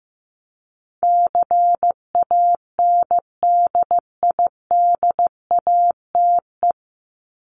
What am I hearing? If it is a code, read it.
Morse code, heard as a keyed tone, about 15 wpm, 700 Hz. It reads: CANDIDATE
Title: Top 1501-1750 Words 15wpm